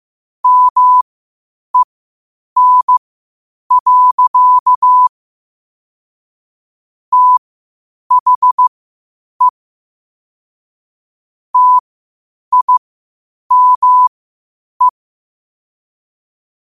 Morse code, heard as a keyed tone, about 15 wpm, spaced as for 8 wpm, 1000 Hz. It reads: MEN. THE TIME
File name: qotdfiles / 8wpm